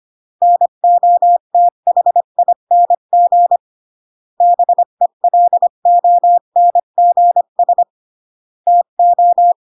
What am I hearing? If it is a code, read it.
Morse code, heard as a keyed tone, about 25 wpm, 700 Hz. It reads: NOTHING BELONGS TO